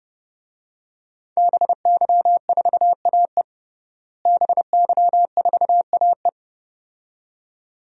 Morse code, heard as a keyed tone, about 32 wpm, 700 Hz. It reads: BY4AE BY4AE
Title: DXCC-100 Call Signs - 5 Characters - Repeated 2x - 30wpm